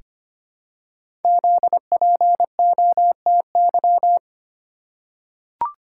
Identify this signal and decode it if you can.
Morse code, heard as a keyed tone, about 25 wpm, 700 Hz. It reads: ZPOTY
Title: Code Groups - 5 Letters - 25wpm